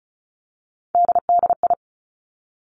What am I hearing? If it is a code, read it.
Morse code, heard as a keyed tone, about 36 wpm, 700 Hz. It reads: DDI
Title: Three Challenging Letter-Number Combinations 35wpm